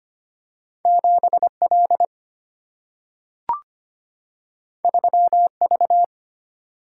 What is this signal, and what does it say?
Morse code, heard as a keyed tone, about 25 wpm, 700 Hz. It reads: 7 L 3 V
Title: Single-Letter Single-Number 25wpm